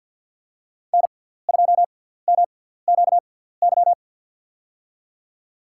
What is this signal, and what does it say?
Morse code, heard as a keyed tone, about 52 wpm, spaced as for 17 wpm, 700 Hz. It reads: N2MQY
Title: USA Call Signs - 5 Characters - 6x Character Spacing - 50wpm